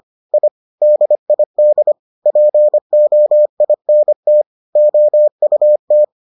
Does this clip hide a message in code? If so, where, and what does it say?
Yes: Morse code from 0.3 to 6.0 s, heard as a keyed tone, about 25 wpm, 600 Hz. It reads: I DID POINT OUT